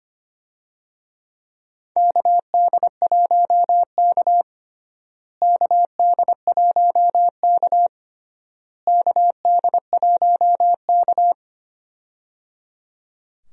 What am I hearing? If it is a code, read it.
Morse code, heard as a keyed tone, about 25 wpm, 700 Hz. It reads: KD1K KD1K KD1K
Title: USA Call Signs - 4 Characters - Repeated 3x - 25wpm